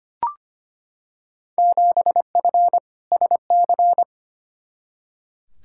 Morse code, heard as a keyed tone, about 25 wpm, 700 Hz. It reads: 7F SC
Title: Field Day Exchanges Encore - 25wpm